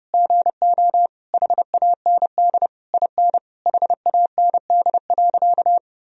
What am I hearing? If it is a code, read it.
Morse code, heard as a keyed tone, about 30 wpm, 700 Hz. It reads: GO HAND IN HAND.
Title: Sentences from Top 1000 Words - Review Part 4 - 30wpm